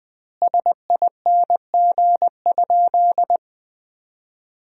Morse code, heard as a keyed tone, about 20 wpm, 700 Hz. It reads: SING?
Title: 2x Word-Spacing - Sentences from Top 400 Words - 20wpm